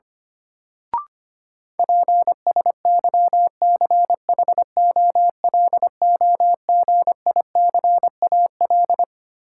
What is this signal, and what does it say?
Morse code, heard as a keyed tone, about 25 wpm, 700 Hz. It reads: PSYCHOLOGICAL
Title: Sentences from Top 1400 Words - New Word Drill - 25wpm